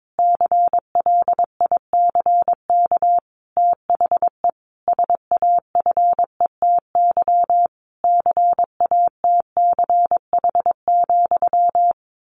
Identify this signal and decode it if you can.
Morse code, heard as a keyed tone, about 22 wpm, 700 Hz. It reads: CLICK THE SAFETY CATCH,
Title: A Sound Of Thunder 22wpm